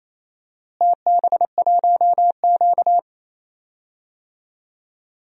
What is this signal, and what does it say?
Morse code, heard as a keyed tone, about 28 wpm, 700 Hz. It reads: TB1Q
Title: DXCC-100 Call Signs - 4 Characters - Repeated 3x - 28wpm